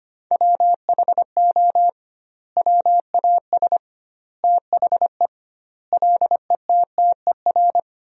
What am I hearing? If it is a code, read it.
Morse code, heard as a keyed tone, about 25 wpm, 700 Hz. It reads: WHO WAS THE LETTER
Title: 2x Word-Spacing - Sentences from Top 300 Words - 25wpm